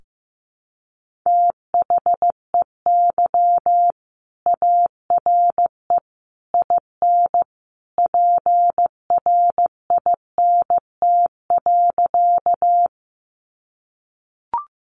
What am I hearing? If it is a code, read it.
Morse code, heard as a keyed tone, about 15 wpm, 700 Hz. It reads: THEY ARE IN PRINT.
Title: Sentences from Top 1400 Words 15wpm